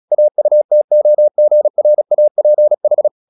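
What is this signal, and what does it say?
Morse code, heard as a keyed tone, about 36 wpm, 600 Hz. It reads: AUTOGRAPH
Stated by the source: Morse practice at 36 wpm, text of Ken Follett's novel Never, from CW med Gnister.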